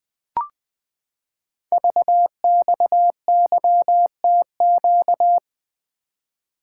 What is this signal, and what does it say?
Morse code, heard as a keyed tone, about 20 wpm, 700 Hz. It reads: VXYTQ
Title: Code Groups - 5 Letter:Numbers - 20wpm